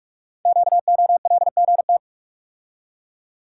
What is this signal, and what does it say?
Morse code, heard as a keyed tone, about 44 wpm, 700 Hz. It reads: QOLGT
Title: Code Groups - 5 Letters - 45wpm